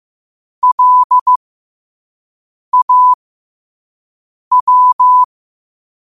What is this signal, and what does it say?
Morse code, heard as a keyed tone, about 15 wpm, spaced as for 5 wpm, 1000 Hz. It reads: LAW
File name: qotdfiles / 5wpm